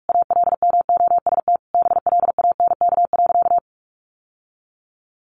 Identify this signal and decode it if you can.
Morse code, heard as a keyed tone, about 44 wpm, 700 Hz. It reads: ALMOST BLANK.
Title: Sentences from Top 1500 Words 45wpm